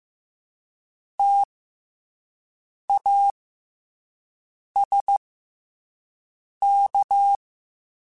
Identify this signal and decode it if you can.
Morse code, heard as a keyed tone, about 15 wpm, 800 Hz. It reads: T A S K
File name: Special INGs-mp3s / Special INGs-mp3s / SlowToFaster ing @10